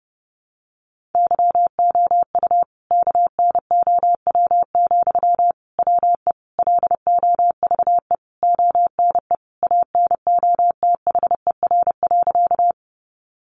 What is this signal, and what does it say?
Morse code, heard as a keyed tone, about 30 wpm, 700 Hz. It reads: YOU KNOW, WE LOVE ONE ANOTHER.
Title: Sentences from Top 600 Words 30wpm